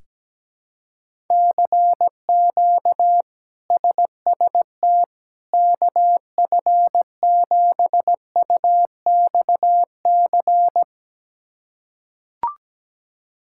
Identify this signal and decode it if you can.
Morse code, heard as a keyed tone, about 17 wpm, 700 Hz. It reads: CQ SST KF7UXC
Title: K1USN SST QSO Practice - 17wpm